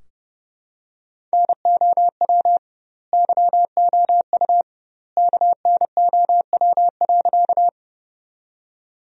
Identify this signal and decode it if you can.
Morse code, heard as a keyed tone, about 30 wpm, 700 Hz. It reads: NOW YOU KNOW.
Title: 2x Word-Spacing - Sentences from Top 100 Words - 30wpm